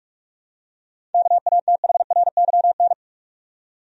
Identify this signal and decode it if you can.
Morse code, heard as a keyed tone, about 45 wpm, 700 Hz. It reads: KATHRYN
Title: USA Common Names - 45wpm